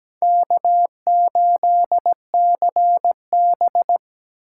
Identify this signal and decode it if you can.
Morse code, heard as a keyed tone, about 20 wpm, 700 Hz. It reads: K8CB
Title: Speed Racing - USA 1x2 Call Signs 17wpm